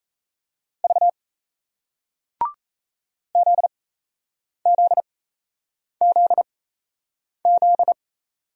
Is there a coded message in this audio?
Yes: Morse code from 0.8 to 7.9 s, heard as a keyed tone, about 38 wpm, spaced as for 8 wpm, 700 Hz. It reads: V ZZZZ